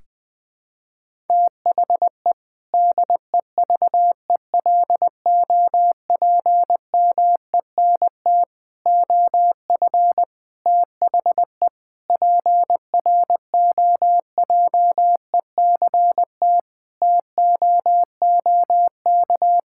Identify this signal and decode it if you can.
Morse code, heard as a keyed tone, about 20 wpm, 700 Hz. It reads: THE DEVELOPMENT OF THE PROJECT TOOK
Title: Sentences with 3-Word Phrases - 20wpm